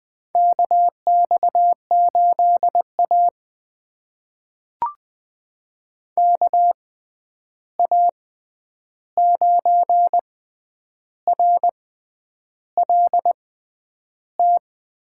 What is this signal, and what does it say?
Morse code, heard as a keyed tone, about 20 wpm, spaced as for 9 wpm, 700 Hz. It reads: KX8A KA9RLT
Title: USA Call Signs - 6x Character Spacing - 20wpm